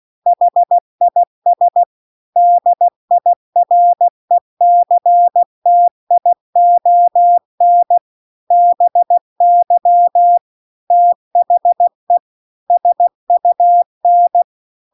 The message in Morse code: HIS DIRECTION BY THE SUN

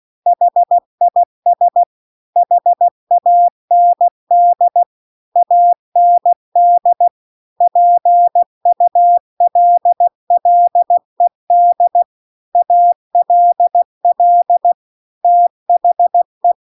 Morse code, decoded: HIS HAND AND PULLED ALL THE

